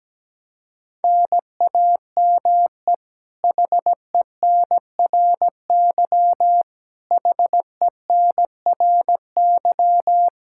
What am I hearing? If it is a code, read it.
Morse code, heard as a keyed tone, about 17 wpm, 700 Hz. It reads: NAME HENRY HENRY
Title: Practice QSO - Exchange 1 - Beginning - 17wpm